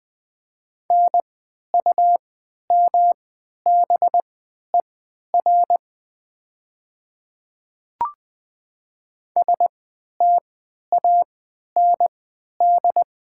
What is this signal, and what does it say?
Morse code, heard as a keyed tone, about 20 wpm, spaced as for 11 wpm, 700 Hz. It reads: NUMBER STAND
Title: 3x Character Spacing - Top 300 Words - 20wpm